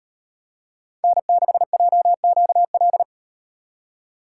Morse code, heard as a keyed tone, about 38 wpm, 700 Hz. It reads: N6JQL